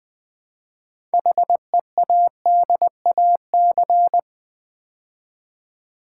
Morse code, heard as a keyed tone, about 20 wpm, 700 Hz. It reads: HEADAC
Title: Build A Word - 2 Letters At A Time - 8-Letter Words - 20wpm